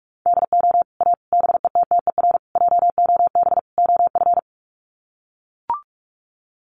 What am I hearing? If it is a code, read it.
Morse code, heard as a keyed tone, about 44 wpm, 700 Hz. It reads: DO A BETTER JOB OF
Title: Sets of 5 Words Encore2 45wpm